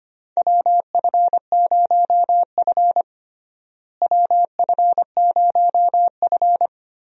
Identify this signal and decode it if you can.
Morse code, heard as a keyed tone, about 25 wpm, 700 Hz. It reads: WF0F WF0F
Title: USA Call Signs - 4 Characters - Repeated 2x - 25wpm